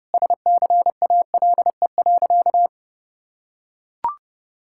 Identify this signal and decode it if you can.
Morse code, heard as a keyed tone, about 29 wpm, 700 Hz. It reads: SCALE.
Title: Sentences from Top 1000 Words - Review Part 5 - 30wpm